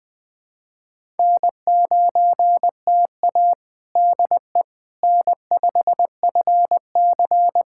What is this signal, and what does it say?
Morse code, heard as a keyed tone, about 20 wpm, 700 Hz. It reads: N9TA DE N5FC
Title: USA Callsign DE Callsign - 4 Characters - 20wpm